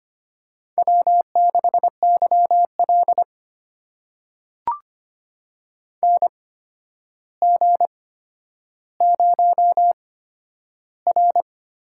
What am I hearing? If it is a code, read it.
Morse code, heard as a keyed tone, about 25 wpm, spaced as for 11 wpm, 700 Hz. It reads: W6YL NG0R
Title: USA Call Signs - 4 Characters - 8x Character Spacing - 25wpm